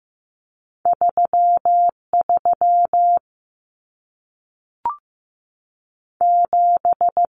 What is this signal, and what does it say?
Morse code, heard as a keyed tone, about 15 wpm, 700 Hz. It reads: 33 7
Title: Numbers - 2-Digit 15wpm